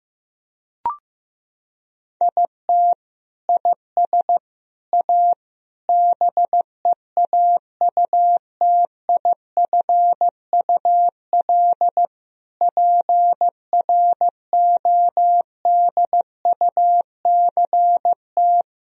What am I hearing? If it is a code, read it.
Morse code, heard as a keyed tone, about 15 wpm, 700 Hz. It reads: IT IS A BEAUTIFUL PRODUCT
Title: Sentences from Top 1000 Words - Review Part 2 - 15wpm